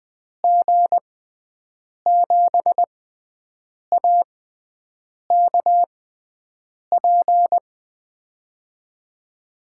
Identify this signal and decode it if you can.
Morse code, heard as a keyed tone, about 20 wpm, spaced as for 7 wpm, 700 Hz. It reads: G7AKP
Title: USA Call Signs - 6 Characters - 6x Character Spacing - 20wpm